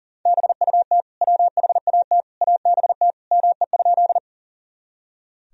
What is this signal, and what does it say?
Morse code, heard as a keyed tone, about 40 wpm, 700 Hz. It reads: BUT WHAT ABT ME?
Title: Sentences from Top 200 Words and Abbreviations 40wpm